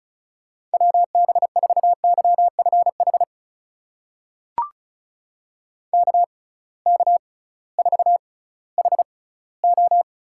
Morse code, heard as a keyed tone, about 36 wpm, spaced as for 19 wpm, 700 Hz. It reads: WB4YFH KK4HO
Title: USA Call Signs - 6 Characters - 6x Character Spacing - 35wpm